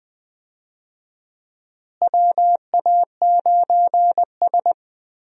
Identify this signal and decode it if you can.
Morse code, heard as a keyed tone, about 20 wpm, 700 Hz. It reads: WA9S
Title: USA Call Signs - 4 Characters - Repeated 3x - 20wpm